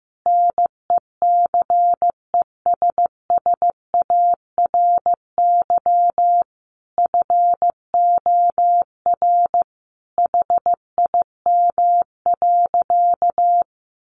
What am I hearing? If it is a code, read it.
Morse code, heard as a keyed tone, about 15 wpm, 700 Hz. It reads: NECESSARY FOR HIM.
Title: Sentences from Top 1500 Words 15wpm